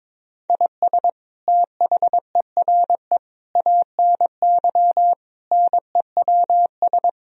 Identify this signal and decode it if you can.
Morse code, heard as a keyed tone, about 22 wpm, 700 Hz. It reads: IS THERE ANY NEWS
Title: Sentences from Top 1000 Words - Review Part 4 - 22wpm